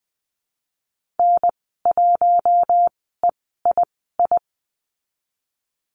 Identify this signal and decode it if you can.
Morse code, heard as a keyed tone, about 20 wpm, spaced as for 14 wpm, 700 Hz. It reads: N1EII
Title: 042 Lesson 8g - Number 1 - Call Signs 20wpm